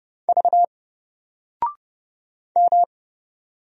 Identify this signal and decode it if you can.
Morse code, heard as a keyed tone, about 30 wpm, 700 Hz. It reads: V M
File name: Single-Letter-Number-30wpm